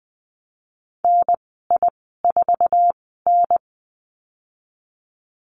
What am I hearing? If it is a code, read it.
Morse code, heard as a keyed tone, about 20 wpm, 700 Hz. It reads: N I 4 N
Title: 055 Lesson 10g - Letter r - Call Signs 20wpm